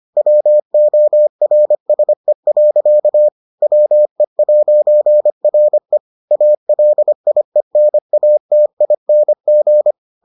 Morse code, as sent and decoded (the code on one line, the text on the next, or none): .-- --- .-. ... .-.-.- / .-- . .----. .-. / .- .-.. .. -. .- - .. -. --.
WORSE. WE'RE ALIENATING